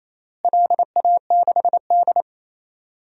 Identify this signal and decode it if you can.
Morse code, heard as a keyed tone, about 28 wpm, 700 Hz. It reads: LA6D
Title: DXCC-100 Call Signs - 4 Characters - Repeated 3x - 28wpm